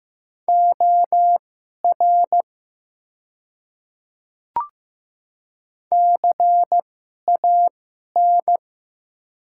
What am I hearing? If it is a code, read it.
Morse code, heard as a keyed tone, about 15 wpm, spaced as for 11 wpm, 700 Hz. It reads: OR CAN
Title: 2x Character Spacing - Top 100 Words - 15wpm